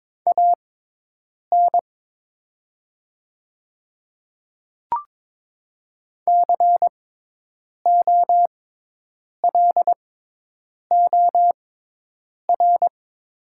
Morse code, heard as a keyed tone, about 22 wpm, spaced as for 8 wpm, 700 Hz. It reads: AN COLOR
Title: 6x Character Spacing - Top 300 Words - 22wpm